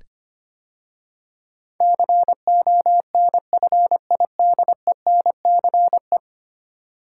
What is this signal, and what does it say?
Morse code, heard as a keyed tone, about 25 wpm, 690 Hz. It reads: CONFIDENCE
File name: 10 Letter Words - First 3 Letters Given - 25wpm